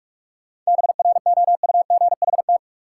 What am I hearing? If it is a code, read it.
Morse code, heard as a keyed tone, about 42 wpm, 700 Hz. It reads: BROUGHT
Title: Sentences from Top 500 Words - New Word Drill - 45wpm